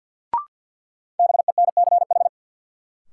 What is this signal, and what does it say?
Morse code, heard as a keyed tone, about 51 wpm, 700 Hz. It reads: BENCH
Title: Sentences from Top 1400 Words - New Word Drill - 50wpm